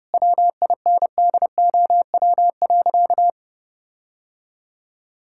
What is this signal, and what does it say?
Morse code, heard as a keyed tone, about 32 wpm, 700 Hz. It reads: WINDOW.
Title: Sentences from Top 1600 Words 30wpm